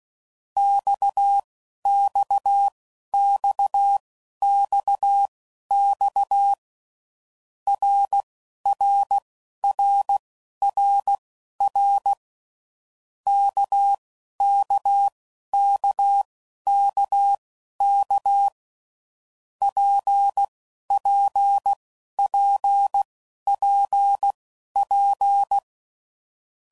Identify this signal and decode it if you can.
Morse code, heard as a keyed tone, about 16 wpm, spaced as for 11 wpm, 800 Hz. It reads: XXXXX RRRRR KKKKK PPPPP
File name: Groups / Basic Letter Groups / RKPX